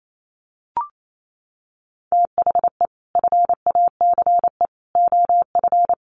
Morse code, heard as a keyed tone, about 28 wpm, 700 Hz. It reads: THE FACE OF
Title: Sets of 3 Words 28wpm